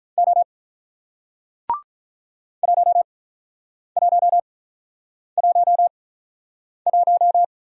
Morse code, heard as a keyed tone, about 47 wpm, spaced as for 9 wpm, 700 Hz. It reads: O 1111